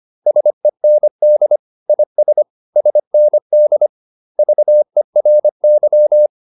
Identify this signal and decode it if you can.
Morse code, heard as a keyed tone, about 25 wpm, 600 Hz. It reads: SEND IS SND VERY
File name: SS319 25